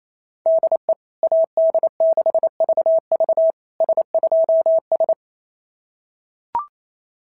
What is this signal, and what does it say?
Morse code, heard as a keyed tone, about 27 wpm, 650 Hz. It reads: DE AD6VV S2S
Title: SOTA QSO Practice 28wpm